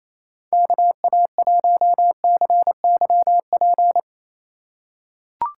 Morse code, heard as a keyed tone, about 27 wpm, 700 Hz. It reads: KA1CYP
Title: USA Call Signs - 6 Characters - 4x Character Spacing - 28wpm